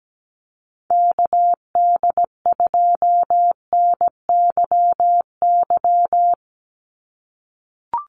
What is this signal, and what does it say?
Morse code, heard as a keyed tone, about 17 wpm, 700 Hz. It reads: KD2NYY
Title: USA Call Signs 17wpm